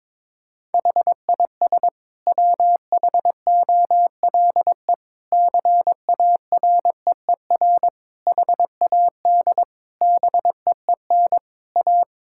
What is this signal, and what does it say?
Morse code, heard as a keyed tone, about 22 wpm, 700 Hz. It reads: HIS WHOLE CAREER HAD BEEN A
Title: Sentences from Top 1000 Words - Review Part 6 - 22wpm